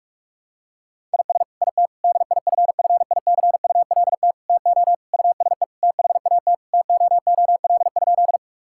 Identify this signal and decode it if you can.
Morse code, heard as a keyed tone, about 41 wpm, 700 Hz. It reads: IS IT DIFFICULT TO USE THAT TOOL?
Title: Sentences from Top 1000 Words - Review Part 6 - 45wpm